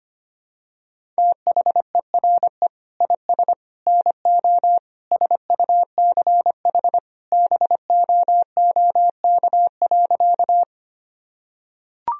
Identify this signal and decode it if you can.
Morse code, heard as a keyed tone, about 25 wpm, 700 Hz. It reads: THERE IS NO SUCH BOOK.
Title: Sentences from Top 1000 Words - Review Part 1 - 25wpm